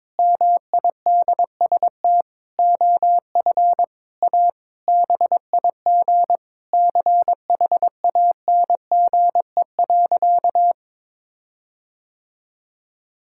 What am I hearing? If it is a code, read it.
Morse code, heard as a keyed tone, about 22 wpm, 700 Hz. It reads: MIDST OF A BIG CHANGE.
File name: Sentences with 7-Word Phrases - 22wpm